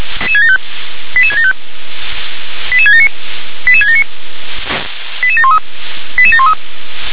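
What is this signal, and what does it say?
Five-tone selective call (ZVEI), heard as quick runs of five tones, tones of about 80 ms. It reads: 88675 88679 88679 88713 88713
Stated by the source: ZVEI five-tone selective call, address 88773, received by SDR.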